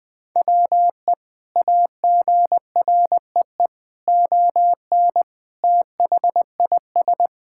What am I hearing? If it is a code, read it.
Morse code, heard as a keyed tone, about 20 wpm, 700 Hz. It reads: WE AGREE ON THIS
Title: Sentences with 2-Word Phrases - 20wpm